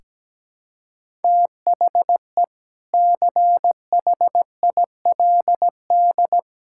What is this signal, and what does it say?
Morse code, heard as a keyed tone, about 17 wpm, 700 Hz. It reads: THE CHILD
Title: Sentences from Top 1000 Words - Review Part 6 - 17wpm